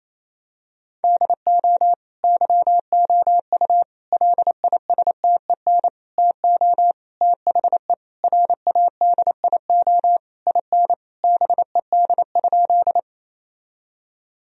Morse code, heard as a keyed tone, about 28 wpm, 700 Hz. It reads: DO YOU LISTEN TO THE RADIO IN BED?
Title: Sentences from Top 1000 Words - Review Part 3 - 28wpm